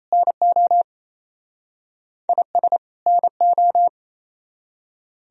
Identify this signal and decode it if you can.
Morse code, heard as a keyed tone, about 36 wpm, 700 Hz. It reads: NO IS NO